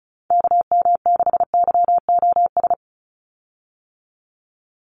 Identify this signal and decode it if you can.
Morse code, heard as a keyed tone, about 34 wpm, 700 Hz. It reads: KM6YOS
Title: USA Call Signs 35wpm